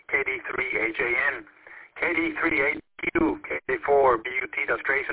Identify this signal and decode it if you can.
Emergency Alert System (SAME) tones, heard as harsh buzzing bursts, with speech. It is heard from 0.5 to 4.7 s.